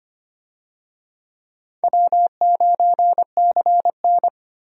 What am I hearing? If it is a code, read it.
Morse code, heard as a keyed tone, about 25 wpm, 700 Hz. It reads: W9CN
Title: USA Call Signs - 4 Characters - 25wpm